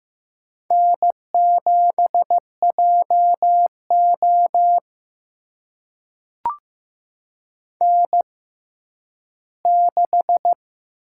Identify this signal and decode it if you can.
Morse code, heard as a keyed tone, about 15 wpm, spaced as for 10 wpm, 700 Hz. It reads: N7JO N6
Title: USA Call Signs - 4 Characters - 6x Character Spacing - 15wpm